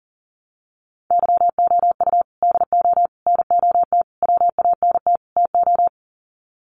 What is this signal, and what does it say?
Morse code, heard as a keyed tone, about 40 wpm, 700 Hz. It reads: YOU DO NOT WANT TO
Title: Sets of 5 Words 40wpm